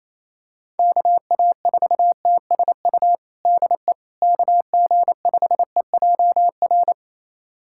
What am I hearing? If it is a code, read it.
Morse code, heard as a keyed tone, about 28 wpm, 700 Hz. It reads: KA4TSU DE KG5EJR
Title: USA Callsign DE Callsign - 6 Characters - 28wpm